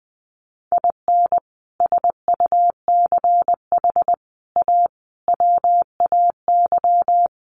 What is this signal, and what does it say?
Morse code, heard as a keyed tone, about 20 wpm, 700 Hz. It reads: IN SUCH A WAY
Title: Sets of 5 Words 20wpm